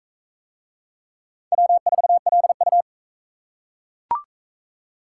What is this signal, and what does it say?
Morse code, heard as a keyed tone, about 41 wpm, 700 Hz. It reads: W4LU